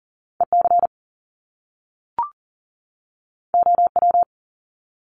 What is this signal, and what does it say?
Morse code, heard as a keyed tone, about 40 wpm, 700 Hz. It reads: EC OW
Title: Top Two Letter Combinations in Words 40wpm